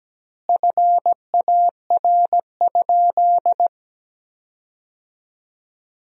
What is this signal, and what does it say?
Morse code, heard as a keyed tone, about 17 wpm, 700 Hz. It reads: FAR?
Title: 2x Word-Spacing - Sentences from Top 300 Words - 17wpm